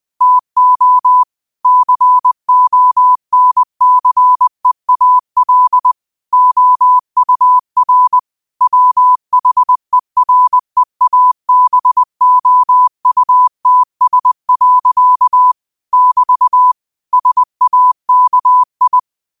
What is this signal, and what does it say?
Morse code, heard as a keyed tone, about 20 wpm, 1000 Hz. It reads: TO CONCEAL OUR WHEREABOUTS. <BT> SAKI
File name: qotdfiles / 20wpm